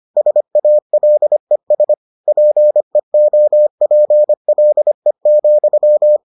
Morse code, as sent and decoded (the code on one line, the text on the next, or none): ... .- .-.. ... / .--. --- .--. .-.. --..--
SALES PEOPLE,